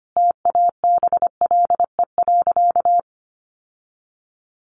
Morse code, heard as a keyed tone, about 25 wpm, 700 Hz. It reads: TABLE.
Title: Sentences from Top 1100 Words 25wpm